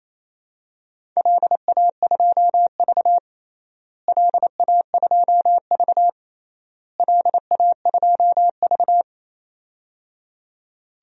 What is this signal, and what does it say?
Morse code, heard as a keyed tone, about 28 wpm, 700 Hz. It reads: LA2V LA2V LA2V
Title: DXCC-100 Call Signs - 4 Characters - Repeated 3x - 28wpm